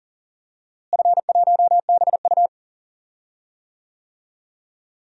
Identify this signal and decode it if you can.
Morse code, heard as a keyed tone, about 40 wpm, 700 Hz. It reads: F1BU
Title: DXCC-100 Call Signs - 4 Characters - Repeated 3x - 40wpm